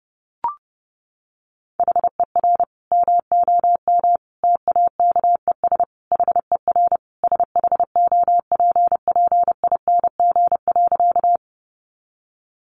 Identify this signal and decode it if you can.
Morse code, heard as a keyed tone, about 30 wpm, 700 Hz. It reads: HER MOM TAKES HER SHOPPING.
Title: Additional Easy Phrases 30wpm